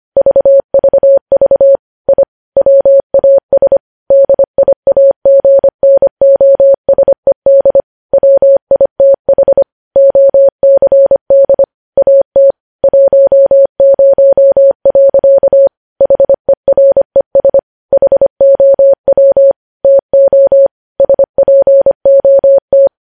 Morse code, heard as a keyed tone, about 25 wpm, 550 Hz. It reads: VVV I WAS DIAGNOSED WITH OCD AT 10. HERES HOW TO SPOT
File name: podcast / News headlines-1x-25x25